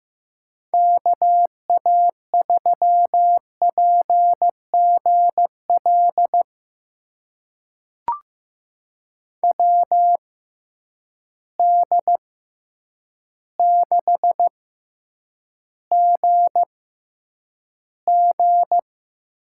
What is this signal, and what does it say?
Morse code, heard as a keyed tone, about 15 wpm, spaced as for 8 wpm, 700 Hz. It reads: KA3PGL WD6GG
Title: USA Call Signs - 6 Characters - 6x Character Spacing - 15wpm